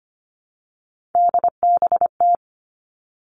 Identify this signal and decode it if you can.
Morse code, heard as a keyed tone, about 25 wpm, 700 Hz. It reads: DBT
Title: Three Challenging Letter-Number Combinations 25wpm